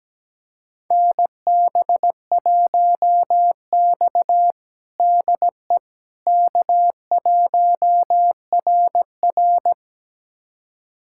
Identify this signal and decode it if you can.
Morse code, heard as a keyed tone, about 17 wpm, 700 Hz. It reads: NB1X DE K1RR
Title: USA Callsign DE Callsign - 4 Characters - 17wpm